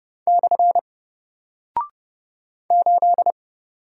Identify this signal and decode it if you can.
Morse code, heard as a keyed tone, about 30 wpm, 700 Hz. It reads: / 8
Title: Single Character (Letter:Number:Punctuation:Prosign) Alt 30wpm